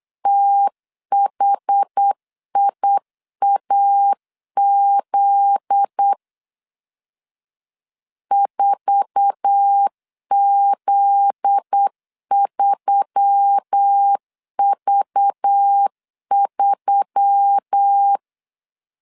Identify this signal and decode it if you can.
Morse code, heard as a keyed tone, about 8 wpm, 800 Hz. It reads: THIAZ 4Z3V3
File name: LEZIONI / 9 2 8wpm 3437